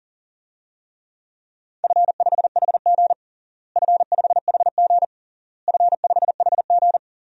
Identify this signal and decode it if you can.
Morse code, heard as a keyed tone, about 40 wpm, 700 Hz. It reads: F5HG F5HG F5HG
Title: DXCC-100 Call Signs - 4 Characters - Repeated 3x - 40wpm